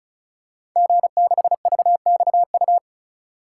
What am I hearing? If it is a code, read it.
Morse code, heard as a keyed tone, about 36 wpm, 700 Hz. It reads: G6VXU
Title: Code Groups - 5 Letter:Numbers - 35wpm